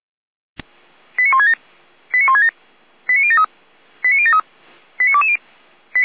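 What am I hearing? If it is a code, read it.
Five-tone selective call (ZVEI), heard as quick runs of five tones, tones of about 70 ms. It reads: 89167 89167 89073 89073 89220
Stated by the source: ZVEI five-tone selective call 89014 received on SDR.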